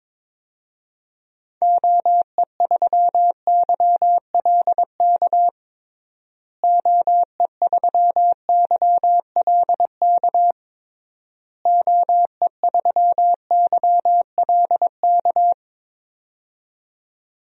Morse code, heard as a keyed tone, about 22 wpm, 700 Hz. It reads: OE3YLK OE3YLK OE3YLK
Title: DXCC-100 Call Signs - 6 Characters - Repeated 3x - 22wpm